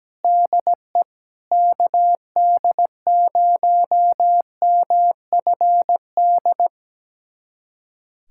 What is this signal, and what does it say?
Morse code, heard as a keyed tone, about 17 wpm, 700 Hz. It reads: DE KD0MFD
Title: USA Callsign DE Callsign 17wpm